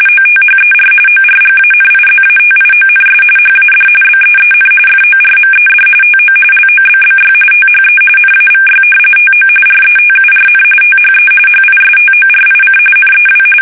RTTY, about 45.45 baud. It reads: OTHER ORGANIZATIONS HAD BEEN FORMED TO LOBBY THE NAVY TO RETURN THE HISTORIC NEW
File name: fsk850 / battleshipnj